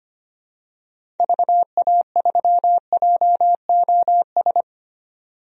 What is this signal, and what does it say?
Morse code, heard as a keyed tone, about 25 wpm, 690 Hz. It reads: VA3JOS
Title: DXCC-100 Call Signs - 6 Characters - 25wpm